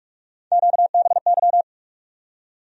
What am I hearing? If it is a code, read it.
Morse code, heard as a keyed tone, about 44 wpm, 700 Hz. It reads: QBY
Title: USA Call Sign Postfixes - 3 Letters 45wpm